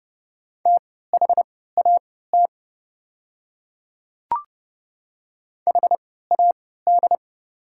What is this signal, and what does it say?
Morse code, heard as a keyed tone, about 30 wpm, spaced as for 17 wpm, 700 Hz. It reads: THAT HAD
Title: 3x Character Spacing - Top 100 Words - 30wpm